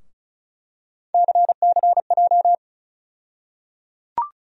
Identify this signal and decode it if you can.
Morse code, heard as a keyed tone, about 36 wpm, 700 Hz. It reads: CCJ
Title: USA Call Sign Postfixes - 3 Letters 35wpm